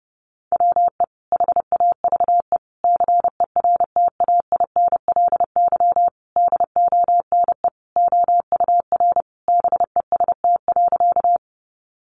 Morse code, heard as a keyed tone, about 30 wpm, 700 Hz. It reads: WE HAVE CERTAINLY DONE OUR BEST.
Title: Sentences from Top 900 Words 30wpm